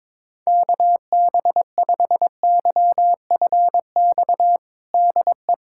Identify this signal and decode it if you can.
Morse code, heard as a keyed tone, about 22 wpm, 700 Hz. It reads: KB5YFX DE
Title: USA Callsign DE Callsign - 6 Characters - 22wpm